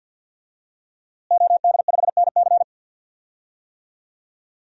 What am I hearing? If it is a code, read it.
Morse code, heard as a keyed tone, about 51 wpm, 700 Hz. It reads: OD5NC
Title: DXCC-100 Call Signs - 5 Characters - Repeated 2x - 50wpm